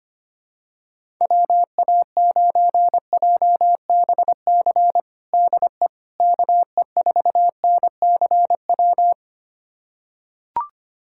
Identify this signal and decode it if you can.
Morse code, heard as a keyed tone, about 25 wpm, 700 Hz. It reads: WA9JBC DE KE4NCW
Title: USA Callsign DE Callsign - 6 Characters - 25wpm